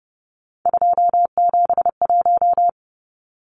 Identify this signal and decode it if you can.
Morse code, heard as a keyed tone, about 30 wpm, 700 Hz. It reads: 271
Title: Numbers - 3-Digit 30wpm